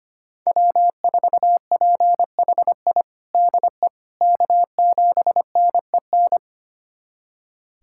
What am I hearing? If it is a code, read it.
Morse code, heard as a keyed tone, about 25 wpm, 700 Hz. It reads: W4PHI DE K7NEN
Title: USA Callsign DE Callsign - 5 Characters - 25wpm